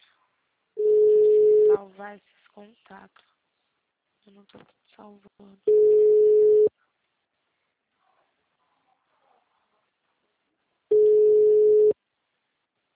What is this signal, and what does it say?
Ringback tone (Europe, 425 Hz), with speech.